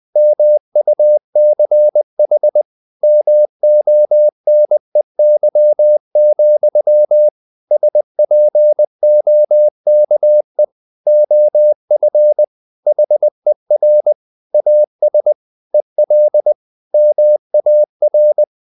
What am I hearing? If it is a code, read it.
Morse code, heard as a keyed tone, about 20 wpm, 600 Hz. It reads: MUCH MONEY, SPOKE OF HER AS EL MAR
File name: the old man and the sea / the old man and the sea part 1 20wpm